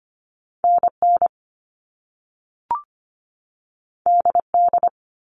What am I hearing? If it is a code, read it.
Morse code, heard as a keyed tone, about 25 wpm, 700 Hz. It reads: NN DD
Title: All Cut Numbers - 2-Digit 25wpm